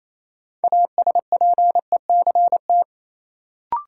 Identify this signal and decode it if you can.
Morse code, heard as a keyed tone, about 27 wpm, 700 Hz. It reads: ASPECT
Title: Sentences from Top 1300 Words - New Word Drill Alt - 28wpm